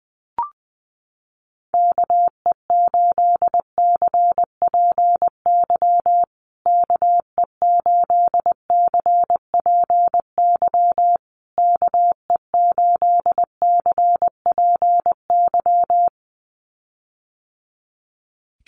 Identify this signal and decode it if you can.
Morse code, heard as a keyed tone, about 20 wpm, 700 Hz. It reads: KE8CPY KE8CPY KE8CPY
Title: USA Call Signs Repeated 3x 20wpm